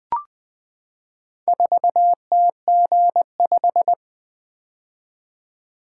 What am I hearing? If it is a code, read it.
Morse code, heard as a keyed tone, about 20 wpm, 700 Hz. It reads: 4TG5
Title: Four Letter-Number Combinations - 20wpm